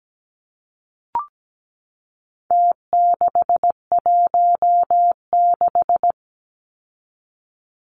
Morse code, heard as a keyed tone, about 17 wpm, 700 Hz. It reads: T616
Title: Numbers - Contest Style - 4-Digit 17wpm